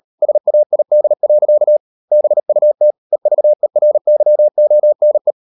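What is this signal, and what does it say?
Morse code, heard as a keyed tone, about 38 wpm, 600 Hz. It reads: SAID. BUT EVERYONE
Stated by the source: Morse code practice at 38 WPM, sending Ken Follett's novel Never in English, from CW med Gnister.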